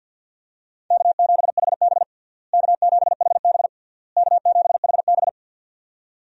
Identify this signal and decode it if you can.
Morse code, heard as a keyed tone, about 49 wpm, 700 Hz. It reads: K7HB K7HB K7HB
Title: USA Call Signs - 4 Characters - Repeated 3x - 50wpm